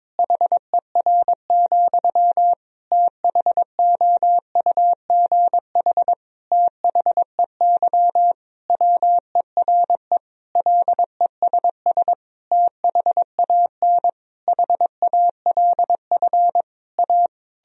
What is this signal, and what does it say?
Morse code, heard as a keyed tone, about 22 wpm, 700 Hz. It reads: HER, THOUGH THEY WERE LESS THAN HALF A
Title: Sentences from Top 1000 Words - Review Part 2 - 22wpm